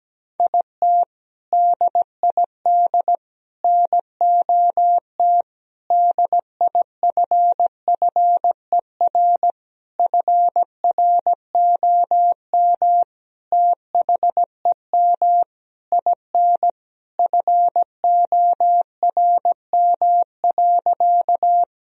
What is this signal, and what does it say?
Morse code, heard as a keyed tone, about 17 wpm, 700 Hz. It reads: IT DID NOT DIFFER FROM THEM IN FORM.
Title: Sentences from Top 1000 Words - Review Part 1 - 17wpm